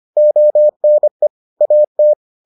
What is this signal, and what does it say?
Morse code, heard as a keyed tone, about 25 wpm, 600 Hz. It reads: ONE AT